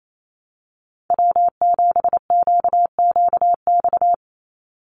Morse code, heard as a keyed tone, about 28 wpm, 700 Hz. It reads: W7QQX
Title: USA Call Signs 28wpm